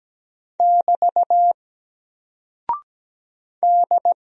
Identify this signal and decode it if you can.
Morse code, heard as a keyed tone, about 17 wpm, 700 Hz. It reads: <BT> D
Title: Single Character (Letter:Number:Punctuation:Prosign) 17wpm